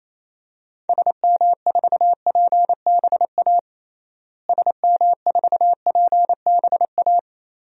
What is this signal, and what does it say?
Morse code, heard as a keyed tone, about 28 wpm, 700 Hz. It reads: SM4PBA SM4PBA
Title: DXCC-100 Call Signs - 6 Characters - Repeated 3x - 28wpm